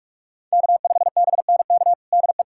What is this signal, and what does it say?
Morse code, heard as a keyed tone, about 43 wpm, 700 Hz. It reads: K5BNK DE
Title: USA Callsign DE Callsign - 5 Characters - 45wpm